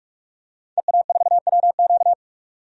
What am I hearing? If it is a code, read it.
Morse code, heard as a keyed tone, about 46 wpm, 700 Hz. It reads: EA4WQ
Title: DXCC-100 Call Signs - 5 Characters - 45wpm